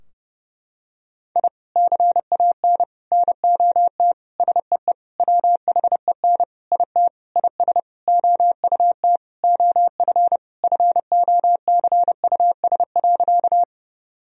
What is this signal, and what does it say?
Morse code, heard as a keyed tone, about 32 wpm, 700 Hz. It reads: I CAN NOT SEE WHEN IT IS OUT OF FOCUS.
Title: Sentences from Top 1000 Words - Review Part 4 - 30wpm